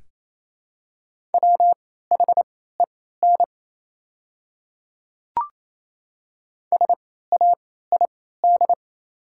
Morse code, heard as a keyed tone, about 28 wpm, spaced as for 16 wpm, 700 Hz. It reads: WHEN SAID
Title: 3x Character Spacing - Top 200 Words - 28wpm